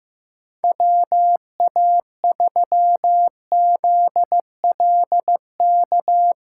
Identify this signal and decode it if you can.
Morse code, heard as a keyed tone, about 15 wpm, 700 Hz. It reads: WA3ZLK
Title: USA Callsign DE Callsign 15wpm